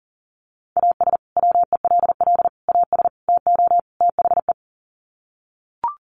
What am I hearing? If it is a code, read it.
Morse code, heard as a keyed tone, about 40 wpm, 700 Hz. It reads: AS WELL AS TO THE
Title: Sets of 5 Words Encore2 40wpm